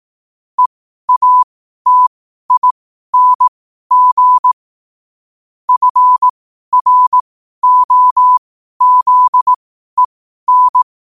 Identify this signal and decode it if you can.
Morse code, heard as a keyed tone, about 18 wpm, spaced as for 12 wpm, 1000 Hz. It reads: EATING FROZEN
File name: qotdfiles / 12wpm